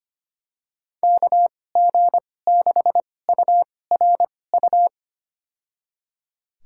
Morse code, heard as a keyed tone, about 25 wpm, spaced as for 18 wpm, 700 Hz. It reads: KG6URU
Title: USA Call Signs - 6 Characters - 2x Character Spacing - 25wpm